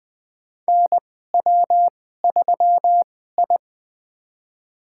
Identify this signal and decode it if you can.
Morse code, heard as a keyed tone, about 20 wpm, spaced as for 14 wpm, 700 Hz. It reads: NW3I
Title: USA Call Signs - 4 Characters - 2x Character Spacing - 20wpm